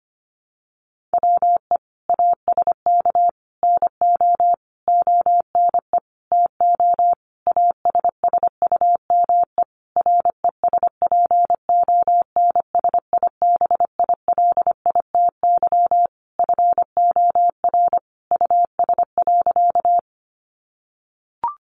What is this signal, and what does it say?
Morse code, heard as a keyed tone, about 25 wpm, 700 Hz. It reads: WE ASK NO ONE TO ASSUME RESPONSIBILITY FOR US.
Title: Sentences from Top 1500 Words 25wpm